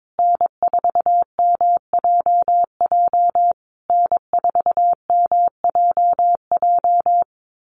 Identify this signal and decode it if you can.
Morse code, heard as a keyed tone, about 22 wpm, 700 Hz. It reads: N4MJJ N4MJJ
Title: USA Call Signs Repeated 3x 22wpm